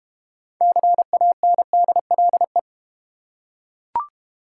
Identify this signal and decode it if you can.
Morse code, heard as a keyed tone, about 30 wpm, 700 Hz. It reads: CANDLE
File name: Easy Words - Speed-Racing with Hints - 25wpm